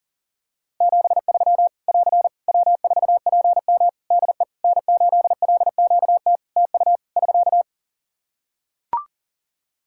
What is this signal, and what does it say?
Morse code, heard as a keyed tone, about 43 wpm, 700 Hz. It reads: 73 <AR> W4PM DE N8LQT TU <SK>
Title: Practice QSO - Exchange 4 - Ending - 40wpm